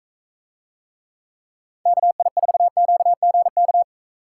Morse code, heard as a keyed tone, about 44 wpm, 700 Hz. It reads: KI4QGK